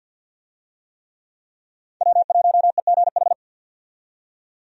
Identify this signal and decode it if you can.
Morse code, heard as a keyed tone, about 50 wpm, 700 Hz. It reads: W1EGH
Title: USA Call Signs - 5 Characters - 50wpm